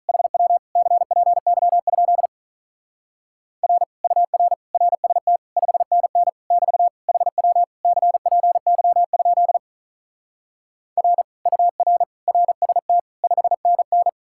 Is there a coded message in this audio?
Yes: Morse code from 0.1 to 14.1 s, heard as a keyed tone, about 46 wpm, 700 Hz. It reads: HW CPY? R UR RST 5NN <BT> HW CPY? R UR RST 5NN